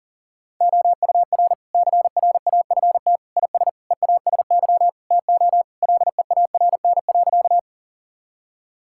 Morse code, heard as a keyed tone, about 40 wpm, 700 Hz. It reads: OUR CRAFT IS EASY TO LEARN.
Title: Sentences from Top 1000 Words - Review Part 6 - 40wpm